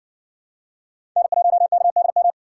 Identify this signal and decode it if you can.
Morse code, heard as a keyed tone, about 60 wpm, 700 Hz. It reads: N1KDM